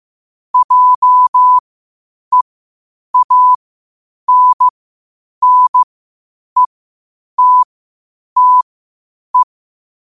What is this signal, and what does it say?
Morse code, heard as a keyed tone, about 15 wpm, spaced as for 8 wpm, 1000 Hz. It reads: JEANNETTE
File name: qotdfiles / 8wpm